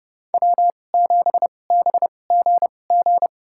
Morse code, heard as a keyed tone, about 30 wpm, spaced as for 22 wpm, 700 Hz. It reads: W7BGG
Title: USA Call Signs - 5 Characters - 2x Character Spacing - 30wpm